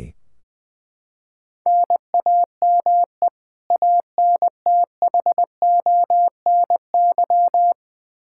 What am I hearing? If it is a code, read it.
Morse code, heard as a keyed tone, about 20 wpm, 700 Hz. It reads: NAME ANTHONY
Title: K1USN SST QSO Practice 20wpm